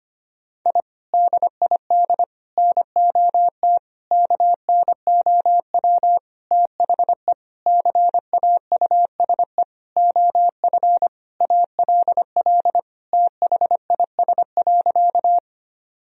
Morse code, heard as a keyed tone, about 25 wpm, 700 Hz. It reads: I DID NOT KNOW THE CAUSE OF ALL THIS.
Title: Sentences from Top 1000 Words - Review Part 1 - 25wpm